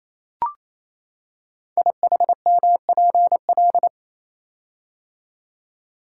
Morse code, heard as a keyed tone, about 27 wpm, 700 Hz. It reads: IHMPL
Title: Code Groups - 5 Letters - 28wpm